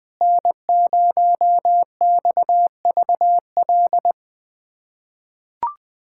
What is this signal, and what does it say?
Morse code, heard as a keyed tone, about 20 wpm, 700 Hz. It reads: N0XVL
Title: USA Callsign DE Callsign 20wpm